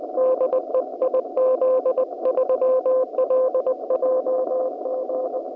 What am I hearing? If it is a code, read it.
Morse code, heard as a keyed tone, about 21 wpm, 520 Hz, with noise. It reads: DE IZ3LJG